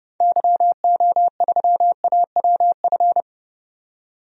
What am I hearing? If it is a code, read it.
Morse code, heard as a keyed tone, about 30 wpm, 700 Hz. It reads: YO3AWF
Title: DXCC-100 Call Signs - 6 Characters - 30wpm